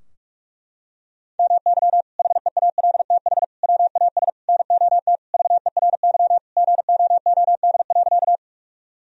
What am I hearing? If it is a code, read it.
Morse code, heard as a keyed tone, about 46 wpm, 700 Hz. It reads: MY HEALTH WAS NOT VERY GOOD.
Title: Sentences from Top 1000 Words - Review Part 3 - 45wpm